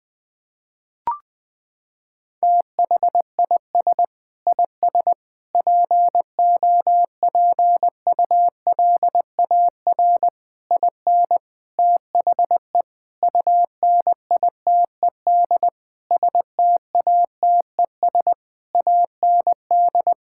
Morse code, heard as a keyed tone, about 20 wpm, 700 Hz. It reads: THIS IS POPULAR IN THE UNITED STATES AND
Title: Sentences with 5-Word Phrases - 20wpm